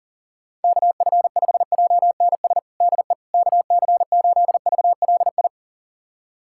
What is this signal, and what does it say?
Morse code, heard as a keyed tone, about 37 wpm, 700 Hz. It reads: KF5JNS DE KC8VLI
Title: USA Callsign DE Callsign - 6 Characters - 40wpm